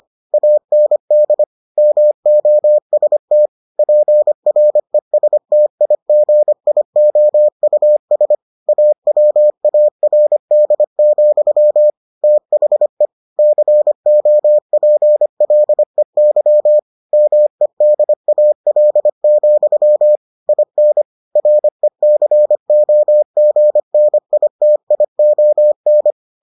Morse code, heard as a keyed tone, about 25 wpm, 600 Hz. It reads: AND MOST PRESTIGIOUS AWARD, THE COPLEY MEDAL, IN RECOGNITION